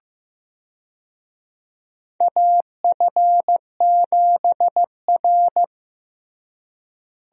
Morse code, heard as a keyed tone, about 15 wpm, 700 Hz. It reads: AF7R
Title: USA Call Signs - 4 Characters - Repeated 2x - 15wpm